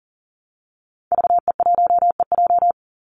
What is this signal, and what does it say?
Morse code, heard as a keyed tone, about 40 wpm, 700 Hz. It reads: VE1EJ
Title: Call Signs Repeated 3x - 40wpm